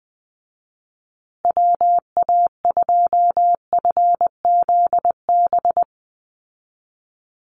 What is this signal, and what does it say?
Morse code, heard as a keyed tone, about 20 wpm, 700 Hz. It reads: WA2FZB
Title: Call Signs Repeated 2x - 20wpm